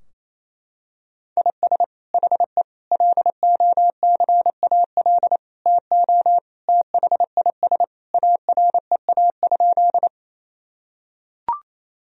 Morse code, heard as a keyed tone, about 28 wpm, 700 Hz. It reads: IS HE LOCAL TO THIS AREA?
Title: Sentences from Top 1000 Words - Review Part 3 - 28wpm